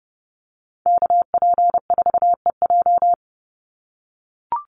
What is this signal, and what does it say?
Morse code, heard as a keyed tone, about 30 wpm, 700 Hz. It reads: KP4EJ
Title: Call Signs - 30wpm